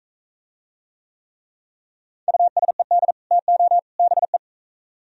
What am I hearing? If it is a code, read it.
Morse code, heard as a keyed tone, about 42 wpm, 700 Hz. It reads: USED TO BE